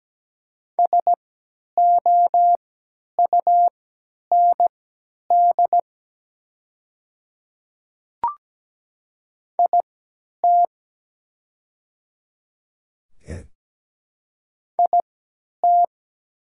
Morse code, heard as a keyed tone, about 17 wpm, spaced as for 10 wpm, 690 Hz. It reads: SOUND IT IT
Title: 3x Character Spacing - Top 100 Words - 17wpm